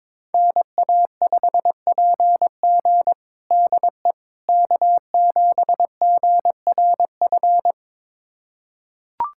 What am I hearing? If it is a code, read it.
Morse code, heard as a keyed tone, about 22 wpm, 700 Hz. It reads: NA5PG DE K7GRF
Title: USA Callsign DE Callsign - 5 Characters - 22wpm